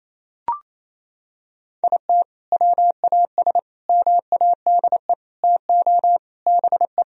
Morse code, heard as a keyed tone, about 28 wpm, 700 Hz. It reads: IT WAS MADE TO BE
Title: Sentences from Top 1000 Words - Review Part 4 - 28wpm